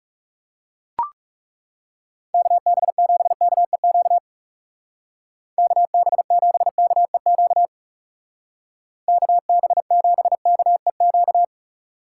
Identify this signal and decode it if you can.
Morse code, heard as a keyed tone, about 40 wpm, 700 Hz. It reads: KB7KEQ KB7KEQ KB7KEQ